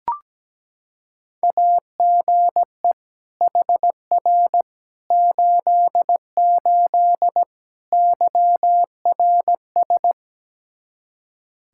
Morse code, heard as a keyed tone, about 17 wpm, 700 Hz. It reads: AGE HR 88 YRS
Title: Practice QSO - Exchange 3 - More Info - 17wpm